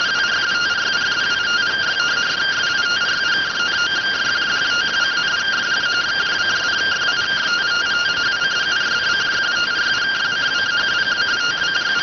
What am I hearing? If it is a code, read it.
Two-tone teleprinter signal, about 170 Hz shift.